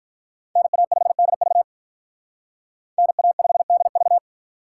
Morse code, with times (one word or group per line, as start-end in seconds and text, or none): 0.6-1.6 NA5DV
3.0-4.2 NA5DV